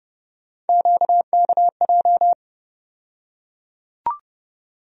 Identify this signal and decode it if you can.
Morse code, heard as a keyed tone, about 30 wpm, 700 Hz. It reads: QKJ
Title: USA Call Sign Postfixes - 3 Letters 30wpm